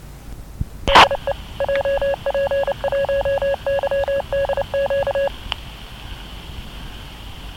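Morse code, heard as a keyed tone, about 29 wpm, 560 Hz, with noise. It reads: DE JP1YDQ
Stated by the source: CW ID "DE JP1YGC" from Japanese repeater JP1YGC on 1292.56 MHz, heard near Mobara, Chiba; dashcam recording with noise mixed in.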